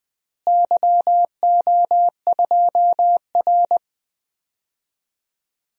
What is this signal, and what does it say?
Morse code, heard as a keyed tone, about 20 wpm, 700 Hz. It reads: YO2R
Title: DXCC-100 Call Signs - 4 Characters - Repeated 3x - 20wpm